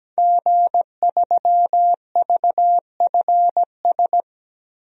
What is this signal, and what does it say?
Morse code, heard as a keyed tone, about 17 wpm, 700 Hz. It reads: G3VFS
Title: DXCC-100 Call Signs - 5 Characters - 17wpm